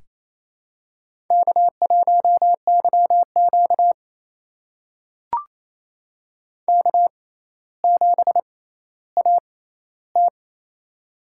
Morse code, heard as a keyed tone, about 29 wpm, spaced as for 14 wpm, 700 Hz. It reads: K1YQ K7AT
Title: USA Call Signs - 4 Characters - 6x Character Spacing - 28wpm